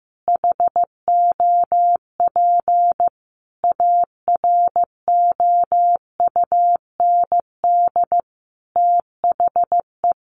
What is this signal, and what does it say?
Morse code, heard as a keyed tone, about 15 wpm, 700 Hz. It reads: HOP AROUND THE
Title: Easy Phrases 15wpm